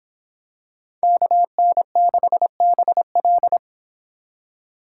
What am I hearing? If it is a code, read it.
Morse code, heard as a keyed tone, about 26 wpm, 700 Hz. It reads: KN6BL